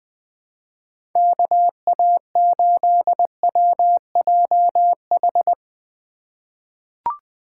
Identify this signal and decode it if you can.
Morse code, heard as a keyed tone, about 20 wpm, 700 Hz. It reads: KA8WJH
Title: USA Call Signs - 6 Characters - 2x Character Spacing - 20wpm